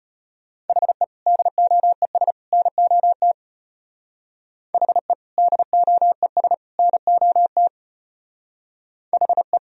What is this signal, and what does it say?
Morse code, heard as a keyed tone, about 32 wpm, 700 Hz. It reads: HE DOES NOT HE DOES NOT HE